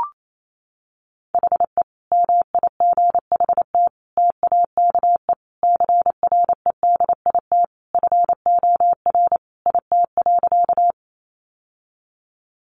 Morse code, heard as a keyed tone, about 28 wpm, 700 Hz. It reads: HE MIGHT TAKE CREDIT FOR IT.
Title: Sentences from Top 1000 Words 28wpm